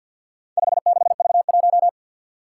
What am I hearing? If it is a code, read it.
Morse code, heard as a keyed tone, about 46 wpm, 700 Hz. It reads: 56V1
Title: Four Challenging Letter-Number Combinations 50wpm